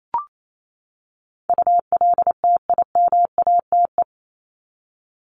Morse code, heard as a keyed tone, about 28 wpm, 700 Hz. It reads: ULTIMATE
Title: LY Suffix - Top 100 Words with Base Words 28wpm